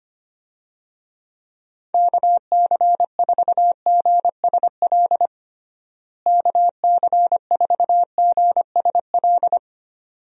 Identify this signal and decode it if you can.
Morse code, heard as a keyed tone, about 25 wpm, 700 Hz. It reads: KC4GSL KC4GSL
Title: USA Call Signs - 6 Characters - Repeated 2x - 25wpm